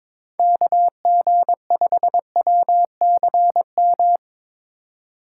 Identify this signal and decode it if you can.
Morse code, heard as a keyed tone, about 21 wpm, 700 Hz. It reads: KG5WCM
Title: USA Call Signs - 6 Characters - 4x Character Spacing - 22wpm